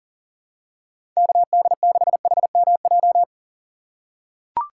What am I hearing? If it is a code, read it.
Morse code, heard as a keyed tone, about 40 wpm, 700 Hz. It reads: KD6HMJ
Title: USA Call Signs - 6 Characters - 2x Character Spacing - 40wpm